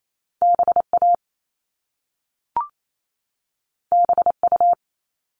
Morse code, heard as a keyed tone, about 28 wpm, 700 Hz. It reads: BA BU
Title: All Cut Numbers - 2-Digit 28wpm